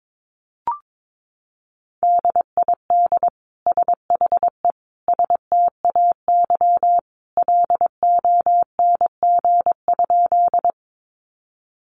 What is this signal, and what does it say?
Morse code, heard as a keyed tone, about 22 wpm, 700 Hz. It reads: DID SHE STAY LONG?
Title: Sentences from Top 500 Words 22wpm